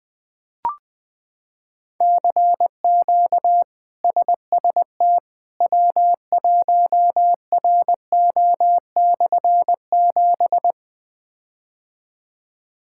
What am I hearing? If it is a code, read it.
Morse code, heard as a keyed tone, about 20 wpm, 700 Hz. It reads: CQ SST W1RO/7
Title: K1USN SST QSO Practice - 20wpm